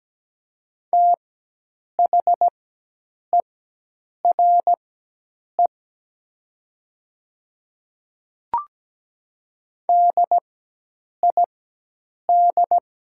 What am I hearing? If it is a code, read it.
Morse code, heard as a keyed tone, about 17 wpm, spaced as for 8 wpm, 700 Hz. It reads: THERE DID
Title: 4x Character Spacing - Top 500 Words - 17wpm